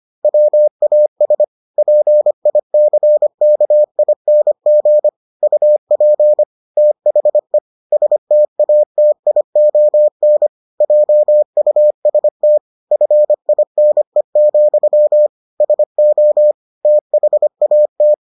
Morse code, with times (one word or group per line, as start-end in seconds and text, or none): 0.2-1.4 WAS
1.8-5.1 PICKING
5.4-6.4 UP
6.8-7.6 THE
7.9-10.5 STATION
10.8-12.6 JUST
12.9-15.3 FINE,
15.6-16.5 SO
16.9-18.1 THAT